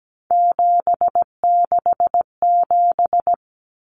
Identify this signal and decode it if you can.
Morse code, heard as a keyed tone, about 17 wpm, 700 Hz. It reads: 767
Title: Common Cut Numbers - 3-Digit 17wpm